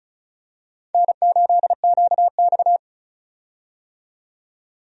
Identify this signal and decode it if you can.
Morse code, heard as a keyed tone, about 36 wpm, 700 Hz. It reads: N8QX
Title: USA Call Signs - 4 Characters - 35wpm